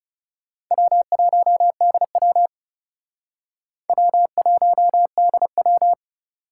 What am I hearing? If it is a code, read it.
Morse code, heard as a keyed tone, about 40 wpm, 700 Hz. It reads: W1DW W1DW